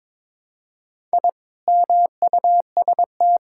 Morse code, heard as a keyed tone, about 22 wpm, 700 Hz. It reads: I MUST
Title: Sentences from Top 1000 Words - Review Part 6 - 22wpm